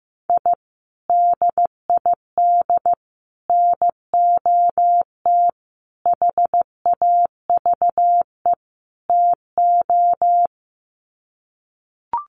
Morse code, heard as a keyed tone, about 15 wpm, 700 Hz. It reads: I DID NOT HAVE TO
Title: Sets of 5 Words 15wpm